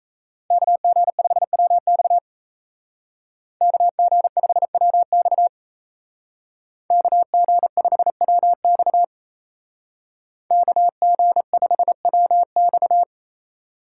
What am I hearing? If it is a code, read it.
Morse code, heard as a keyed tone, about 36 wpm, 700 Hz. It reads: KG5WX KG5WX KG5WX KG5WX